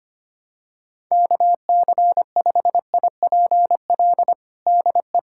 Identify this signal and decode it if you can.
Morse code, heard as a keyed tone, about 25 wpm, 700 Hz. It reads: KC5IPL DE
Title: USA Callsign DE Callsign - 6 Characters - 25wpm